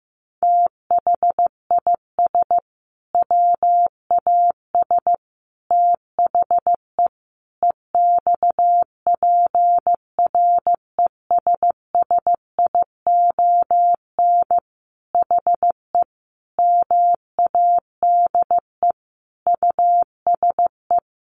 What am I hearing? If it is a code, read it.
Morse code, heard as a keyed tone, about 15 wpm, 700 Hz. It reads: THIS WAS THE EXPRESSION HE MADE USE
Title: Sentences from Top 1200 Words 15wpm